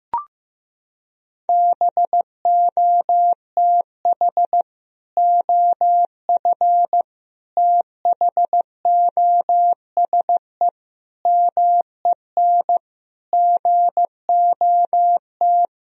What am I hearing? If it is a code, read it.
Morse code, heard as a keyed tone, about 15 wpm, 700 Hz. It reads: BOTH OF THOSE MEN GOT
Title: Sentences from Top 1000 Words - Review Part 1 - 15wpm